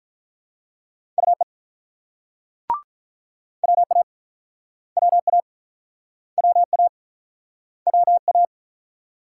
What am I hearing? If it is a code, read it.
Morse code, heard as a keyed tone, about 46 wpm, 700 Hz. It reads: UE WA WA WA WA